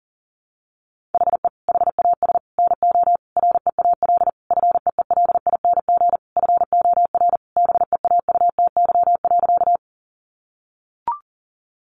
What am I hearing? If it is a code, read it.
Morse code, heard as a keyed tone, about 40 wpm, 700 Hz. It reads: HE HAS NO REAL FEELING FOR BEAUTY.
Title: Sentences from Top 1200 Words 40wpm